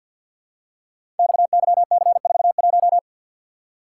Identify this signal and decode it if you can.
Morse code, heard as a keyed tone, about 46 wpm, 690 Hz. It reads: XYC41
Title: Code Groups - 5 Letter:Numbers - 50wpm